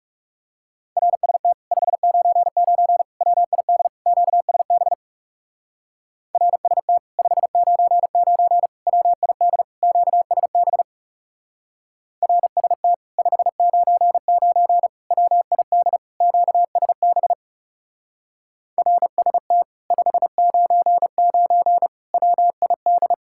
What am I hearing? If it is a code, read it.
Morse code, heard as a keyed tone, about 37 wpm, 700 Hz. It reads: RST 599 WID QSB RST 599 WID QSB RST 599 WID QSB RST 599 WID
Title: Speed-Racing Common QSO Phrases - 30wpm